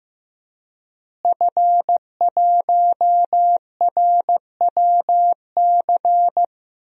Morse code, heard as a keyed tone, about 15 wpm, 700 Hz. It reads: F1RWC
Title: DXCC-100 Call Signs - 5 Characters - Repeated 3x - 15wpm